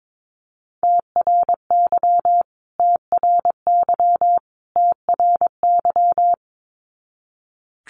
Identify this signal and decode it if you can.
Morse code, heard as a keyed tone, about 22 wpm, 700 Hz. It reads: TRY TRY TRY
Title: Top 300 Words Repeated 22wpm